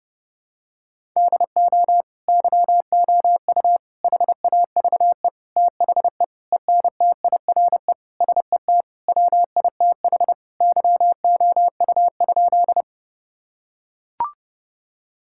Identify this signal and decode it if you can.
Morse code, heard as a keyed tone, about 30 wpm, 700 Hz. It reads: DO YOU HAVE THE ENTIRE SET WITH YOU?
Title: Sentences from Top 1000 Words - Review Part 5 - 30wpm